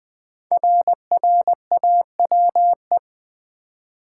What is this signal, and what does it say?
Morse code, heard as a keyed tone, about 20 wpm, 700 Hz. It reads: RRAWE
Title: Code Groups - 5 Letters - 20wpm